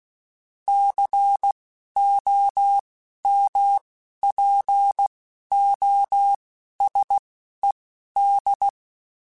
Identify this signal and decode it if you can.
Morse code, heard as a keyed tone, about 16 wpm, spaced as for 11 wpm, 800 Hz. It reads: COMPOSED
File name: Technical Subjects / Technical Subjects-mp3s / Sw Pwr Sup @10